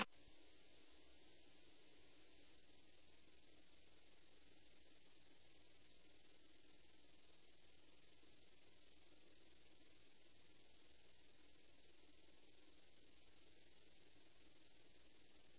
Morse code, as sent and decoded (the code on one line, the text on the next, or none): none